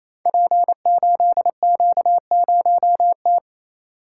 Morse code, heard as a keyed tone, about 27 wpm, 700 Hz. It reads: P8Q0T
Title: Code Groups - 5 Letter:Numbers - 28wpm